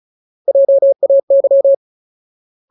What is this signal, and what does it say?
Morse code, heard as a keyed tone, about 34 wpm, 550 Hz. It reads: JAY
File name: Ultimate Word List / 35WPM Ultimate